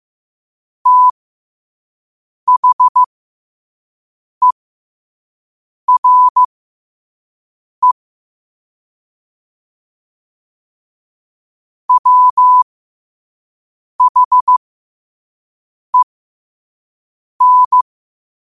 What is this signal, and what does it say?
Morse code, heard as a keyed tone, about 15 wpm, spaced as for 5 wpm, 1000 Hz. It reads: THERE WHEN
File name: qotdfiles / 5wpm